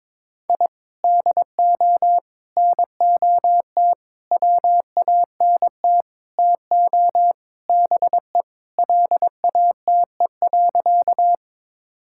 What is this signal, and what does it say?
Morse code, heard as a keyed tone, about 22 wpm, 700 Hz. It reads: I DO NOT WANT TO BE LATE.
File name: practice / 22wpm / Sentences with 4-Word Phrases - 22wpm